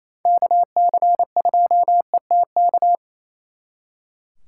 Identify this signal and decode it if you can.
Morse code, heard as a keyed tone, about 34 wpm, 700 Hz. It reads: KC2ETK
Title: Speed Racing - USA Call Signs 28wpm